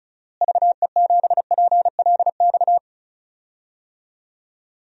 Morse code, heard as a keyed tone, about 36 wpm, 700 Hz. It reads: VE7PLX
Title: DXCC-100 Call Signs - 6 Characters - Repeated 2x - 35wpm